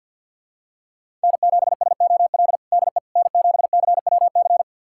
Morse code, heard as a keyed tone, about 45 wpm, 700 Hz. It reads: N7SOL DE N7CWC
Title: USA Callsign DE Callsign - 5 Characters - 50wpm